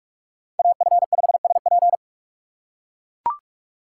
Morse code, heard as a keyed tone, about 44 wpm, 700 Hz. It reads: AF5SP